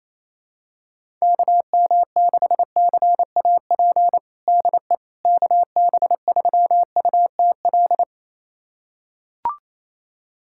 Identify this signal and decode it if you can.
Morse code, heard as a keyed tone, about 28 wpm, 700 Hz. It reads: KM6CAP DE KB3UTL
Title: USA Callsign DE Callsign 28wpm